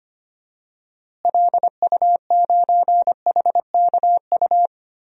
Morse code, heard as a keyed tone, about 25 wpm, 700 Hz. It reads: LU9HKU
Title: DXCC-100 Call Signs - 6 Characters - Repeated 3x - 25wpm